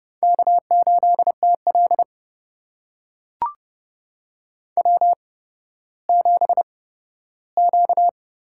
Morse code, heard as a keyed tone, about 30 wpm, spaced as for 14 wpm, 700 Hz. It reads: K8TL W7Q
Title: USA Call Signs - 4 Characters - 8x Character Spacing - 30wpm